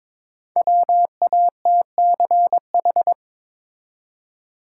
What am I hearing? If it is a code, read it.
Morse code, heard as a keyed tone, about 22 wpm, 700 Hz. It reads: WATCH
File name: 5 Letter Words - First 3 Letters Given - 22wpm